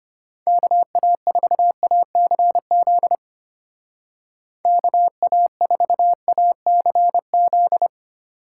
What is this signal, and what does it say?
Morse code, heard as a keyed tone, about 31 wpm, 700 Hz. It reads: KA4ACZ KA4ACZ
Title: Speed Racing - USA 2x3 Call Signs 25wpm